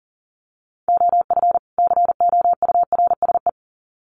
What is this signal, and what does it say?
Morse code, heard as a keyed tone, about 40 wpm, 700 Hz. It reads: OF COURSE
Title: Sets of 2 Words 40wpm